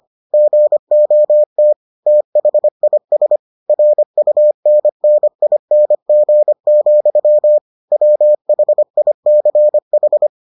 Morse code, heard as a keyed tone, about 25 wpm, 600 Hz. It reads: GOT THIS RUNNING, WHICH